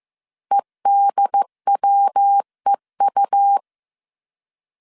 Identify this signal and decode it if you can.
Morse code, heard as a keyed tone, about 15 wpm, 800 Hz. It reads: EDWEU